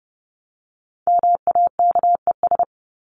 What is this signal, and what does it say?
Morse code, heard as a keyed tone, about 30 wpm, 700 Hz. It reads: MAKES
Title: ES:S Suffix - Top 100 Words with Base Words 30wpm